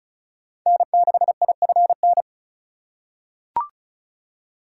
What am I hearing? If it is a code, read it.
Morse code, heard as a keyed tone, about 36 wpm, 700 Hz. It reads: N6IFN
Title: USA Call Signs - 5 Characters - 3x Character Spacing - 35wpm